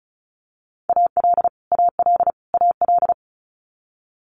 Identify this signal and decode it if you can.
Morse code, heard as a keyed tone, about 36 wpm, 700 Hz. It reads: AL AL AL
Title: Multi-letter Phonemes Repeated 3x 35wpm